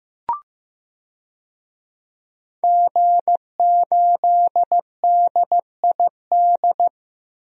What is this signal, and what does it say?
Morse code, heard as a keyed tone, about 15 wpm, 700 Hz. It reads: G8DID
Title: DXCC-100 Call Signs - 5 Characters - Repeated 3x - 15wpm